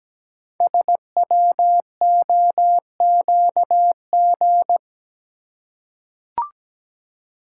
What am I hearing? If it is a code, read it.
Morse code, heard as a keyed tone, about 17 wpm, 700 Hz. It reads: SWOQG
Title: Code Groups - 5 Letters - 17wpm